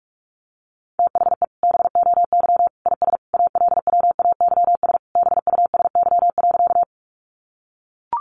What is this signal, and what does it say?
Morse code, heard as a keyed tone, about 44 wpm, 700 Hz. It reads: THE BOY IS ALWAYS BUSY.
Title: Additional Easy Phrases 45wpm